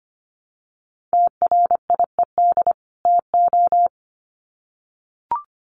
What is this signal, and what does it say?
Morse code, heard as a keyed tone, about 25 wpm, 700 Hz. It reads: TRIED TO
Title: Sets of 2 Words 25wpm